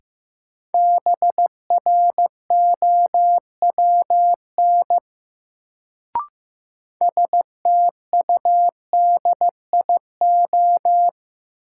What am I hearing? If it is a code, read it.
Morse code, heard as a keyed tone, about 15 wpm, 700 Hz. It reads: BROWN STUDIO
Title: Sentences from Top 1200 Words - New Word Drill Alt - 15wpm